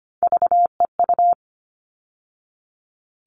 Morse code, heard as a keyed tone, about 25 wpm, 700 Hz. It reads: VEU
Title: All Cut Numbers - 3-Digit 25wpm